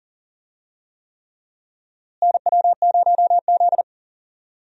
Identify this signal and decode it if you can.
Morse code, heard as a keyed tone, about 38 wpm, 700 Hz. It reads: NW0Z
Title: USA Call Signs - 4 Characters - Repeated 3x - 40wpm